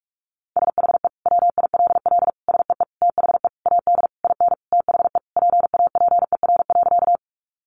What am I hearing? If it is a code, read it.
Morse code, heard as a keyed tone, about 44 wpm, 700 Hz. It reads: SHE WILL SEE THE AD IN THE PAPER.
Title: Sentences from Top 1300 Words 45wpm